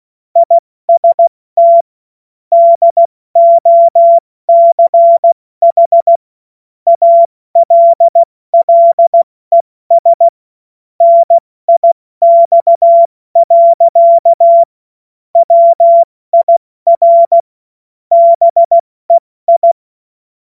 Morse code, heard as a keyed tone, about 16 wpm, 680 Hz. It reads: IST DOCH ALLES NIX. WIR BEI
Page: cw-witze